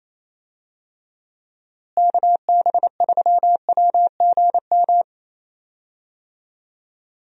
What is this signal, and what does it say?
Morse code, heard as a keyed tone, about 28 wpm, 700 Hz. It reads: KB3WGM
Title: USA Call Signs - 6 Characters - Repeated 3x - 28wpm